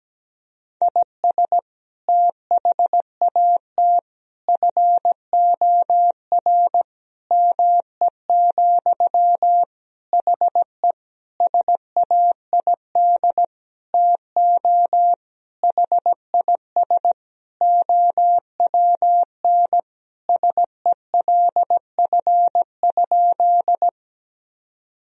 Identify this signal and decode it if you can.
Morse code, heard as a keyed tone, about 17 wpm, 700 Hz. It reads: IS THAT FOR ME, HE SAID TO HIS OWN SELF?
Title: Sentences from Top 1000 Words - Review Part 1 - 17wpm